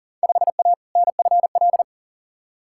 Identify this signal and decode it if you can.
Morse code, heard as a keyed tone, about 38 wpm, 700 Hz. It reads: 5A NFL
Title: Field Day Exchanges - 40wpm